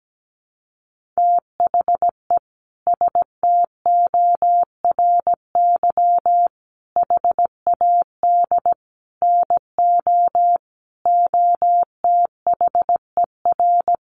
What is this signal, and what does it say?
Morse code, heard as a keyed tone, about 17 wpm, 700 Hz. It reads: THE STORY HAD NO OTHER
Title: Sentences from Top 1200 Words 17wpm